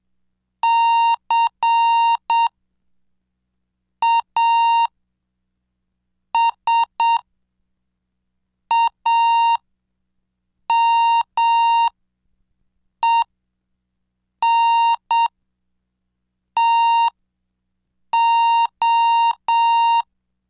Morse code, heard as a keyed tone, about 7 wpm, spaced as for 4 wpm, 930 Hz. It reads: CASAMENTO
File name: cursos / CDestudo / 17 Track 17